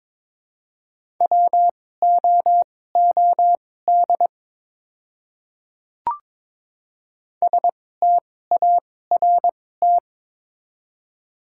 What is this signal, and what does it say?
Morse code, heard as a keyed tone, about 23 wpm, spaced as for 16 wpm, 700 Hz. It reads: WOOD START
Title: 2x Character Spacing - Top 300 Words - 22wpm